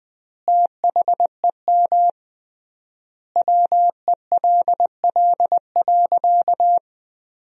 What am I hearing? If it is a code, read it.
Morse code, heard as a keyed tone, about 20 wpm, 700 Hz. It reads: THEM WELL.
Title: 3x Word-Spacing - Sentences from Top 300 Words - 20wpm